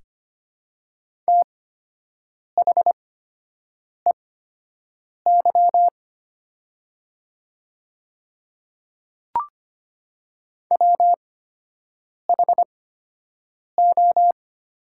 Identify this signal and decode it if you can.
Morse code, heard as a keyed tone, about 25 wpm, spaced as for 7 wpm, 700 Hz. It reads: THEY WHO
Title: 8x Character Spacing - Top 100 Words - 25wpm